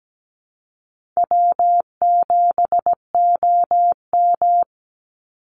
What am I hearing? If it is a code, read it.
Morse code, heard as a keyed tone, about 17 wpm, 700 Hz. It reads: W7OM
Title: Easy Call Signs Repeated 2x - 17wpm